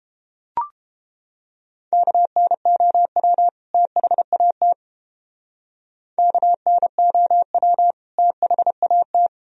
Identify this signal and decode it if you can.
Morse code, heard as a keyed tone, about 30 wpm, 700 Hz. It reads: KNOW THAT KNOW THAT